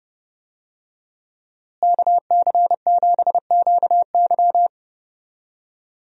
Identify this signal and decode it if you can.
Morse code, heard as a keyed tone, about 31 wpm, 700 Hz. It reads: KC7QY
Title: USA Call Signs - 5 Characters - Repeated 2x - 30wpm